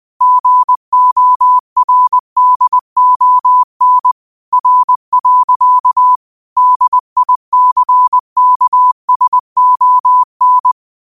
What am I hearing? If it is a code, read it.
Morse code, heard as a keyed tone, about 20 wpm, 1000 Hz. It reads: GORDON R. DICKSON
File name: qotdfiles / 20wpm